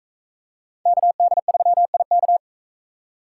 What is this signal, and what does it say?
Morse code, heard as a keyed tone, about 38 wpm, 700 Hz. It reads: KD3IK